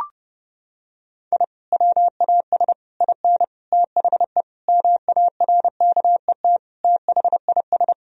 Morse code, heard as a keyed tone, about 32 wpm, 700 Hz. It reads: I WAS IN THE MARKET THIS
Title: Sentences from Top 1000 Words - Review Part 3 - 30wpm